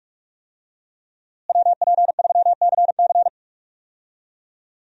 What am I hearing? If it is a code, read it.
Morse code, heard as a keyed tone, about 41 wpm, 700 Hz. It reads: WP3CC
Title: USA Call Signs - 5 Characters - Repeated 2x - 45wpm